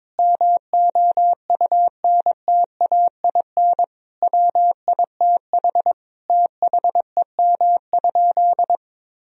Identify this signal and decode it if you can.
Morse code, heard as a keyed tone, about 22 wpm, 690 Hz. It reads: MOUNTAIN WITH THEM?
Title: Sentences from Top 1000 Words - Review Part 3 - 22wpm